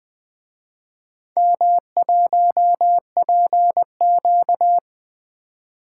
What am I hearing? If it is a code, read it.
Morse code, heard as a keyed tone, about 24 wpm, 700 Hz. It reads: M1PQ
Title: Speed-Racing - Four Letter-Number Combinations - 20wpm